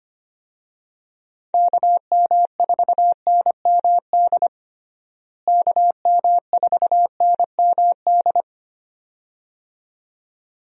Morse code, heard as a keyed tone, about 25 wpm, 700 Hz. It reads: KM4NMD KM4NMD
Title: USA Call Signs - 6 Characters - Repeated 2x - 25wpm